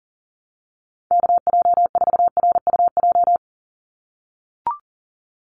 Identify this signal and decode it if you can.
Morse code, heard as a keyed tone, about 39 wpm, 700 Hz. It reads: KJ4RUJ
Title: USA Call Signs 40wpm